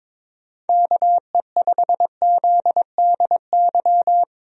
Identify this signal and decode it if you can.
Morse code, heard as a keyed tone, about 22 wpm, 700 Hz. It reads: KE5ZDY
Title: USA Call Signs - 6 Characters - Repeated 3x - 22wpm